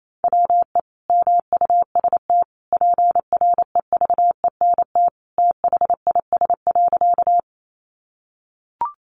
Morse code, heard as a keyed tone, about 28 wpm, 700 Hz. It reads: WE MUST PREVENT THIS.
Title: Sentences from Top 1100 Words 28wpm